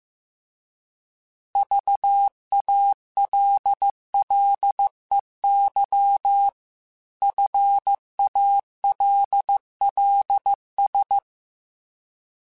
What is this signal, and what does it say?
Morse code, heard as a keyed tone, about 15 wpm, 800 Hz. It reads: VALLEY FALLS